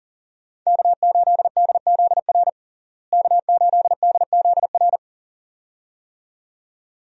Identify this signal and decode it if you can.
Morse code, heard as a keyed tone, about 40 wpm, 700 Hz. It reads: K8DZR K8DZR
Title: USA Call Signs - 5 Characters - Repeated 2x - 40wpm